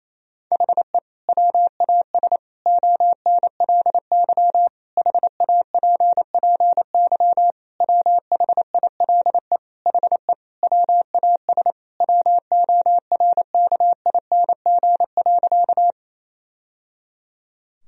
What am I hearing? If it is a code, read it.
Morse code, heard as a keyed tone, about 27 wpm, 700 Hz. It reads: HE WAS ONLY HAPPY WHILE HE WAS WORKING.
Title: Sentences from Top 1000 Words - Review Part 4 - 28wpm